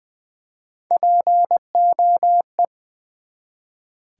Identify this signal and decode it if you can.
Morse code, heard as a keyed tone, about 20 wpm, 700 Hz. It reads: POE
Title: Build A Word - 1 Letter At A Time - 3 to 5-Letter Words - 20wpm